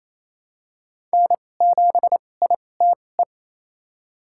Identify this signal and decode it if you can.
Morse code, heard as a keyed tone, about 28 wpm, spaced as for 20 wpm, 700 Hz. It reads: N7ITE
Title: USA Call Signs - 5 Characters - 2x Character Spacing - 28wpm